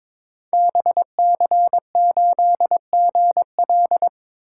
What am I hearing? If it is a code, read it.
Morse code, heard as a keyed tone, about 22 wpm, 700 Hz. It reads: BC8GL
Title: Code Groups - 5 Letter:Numbers - 22wpm